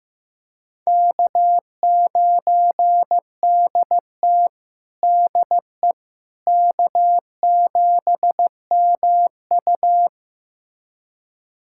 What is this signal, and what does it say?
Morse code, heard as a keyed tone, about 15 wpm, 700 Hz. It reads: K9DT DE K7MU
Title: USA Callsign DE Callsign - 4 Characters - 15wpm